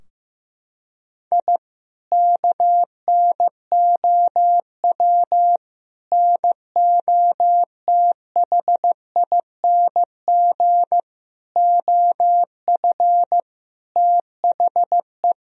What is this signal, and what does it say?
Morse code, heard as a keyed tone, about 15 wpm, 700 Hz. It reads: I KNOW NOTHING OF THE
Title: Sentences from Top 1000 Words - Review Part 3 - 15wpm